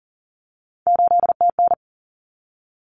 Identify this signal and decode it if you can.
Morse code, heard as a keyed tone, about 40 wpm, 700 Hz. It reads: 8TN
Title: Common Cut Numbers - 3-Digit 40wpm